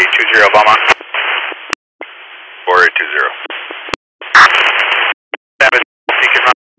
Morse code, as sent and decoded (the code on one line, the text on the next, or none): none